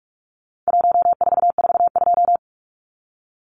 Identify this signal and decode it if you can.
Morse code, heard as a keyed tone, about 44 wpm, 700 Hz. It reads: 1442
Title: Numbers - 4-Digit 45wpm